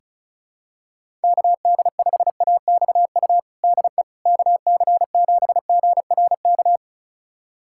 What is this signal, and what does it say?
Morse code, heard as a keyed tone, about 34 wpm, 700 Hz. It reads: KD5AXU DE KC7GRK
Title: USA Callsign DE Callsign - 6 Characters - 35wpm